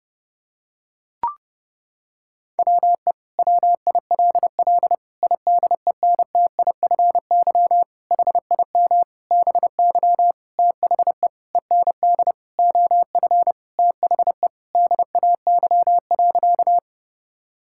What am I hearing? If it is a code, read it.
Morse code, heard as a keyed tone, about 32 wpm, 700 Hz. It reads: WE WILL IDENTIFY HIM BY THE END OF THE DAY.
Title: Sentences from Top 1000 Words - Review Part 6 - 30wpm